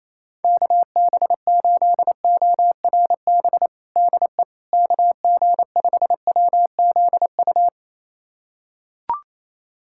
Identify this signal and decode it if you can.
Morse code, heard as a keyed tone, about 28 wpm, 700 Hz. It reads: KB8ORB DE KG5WZU
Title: USA Callsign DE Callsign - 6 Characters - 28wpm